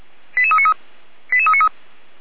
Five-tone selective call (ZVEI), heard as quick runs of five tones, tones of about 70 ms. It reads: 88393 88393
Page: SDR/ZVEI